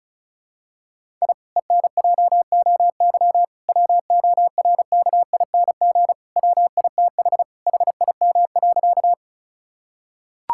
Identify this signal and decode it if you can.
Morse code, heard as a keyed tone, about 36 wpm, 700 Hz. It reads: I ENJOY WORKING WITH HIM.
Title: Sentences from Top 1000 Words - Review Part 5 - 35wpm